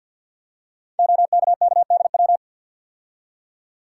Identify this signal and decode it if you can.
Morse code, heard as a keyed tone, about 50 wpm, 700 Hz. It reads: OKKDW
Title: Code Groups - 5 Letter:Numbers - 50wpm